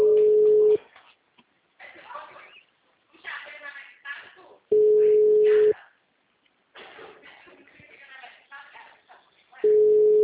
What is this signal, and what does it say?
Ringback tone (Europe, 425 Hz), with speech.
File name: out- -2008- - - .458